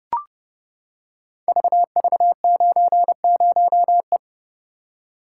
Morse code, heard as a keyed tone, about 30 wpm, 700 Hz. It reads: VV90E
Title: Code Groups - 5 Letter:Numbers - 30wpm